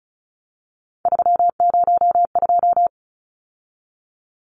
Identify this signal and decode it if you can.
Morse code, heard as a keyed tone, about 34 wpm, 700 Hz. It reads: 302
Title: Numbers - 3-Digit 35wpm